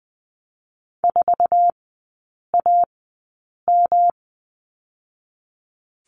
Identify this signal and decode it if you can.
Morse code, heard as a keyed tone, about 20 wpm, 700 Hz. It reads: 4 A M
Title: 212 Lesson 34e - Number 8 - Three Characters 20wpm